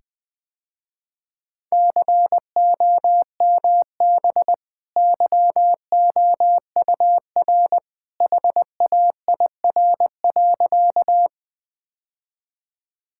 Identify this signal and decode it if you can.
Morse code, heard as a keyed tone, about 20 wpm, 700 Hz. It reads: COMB YOUR HAIR.
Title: Easy Sentences with Hints - 20wpm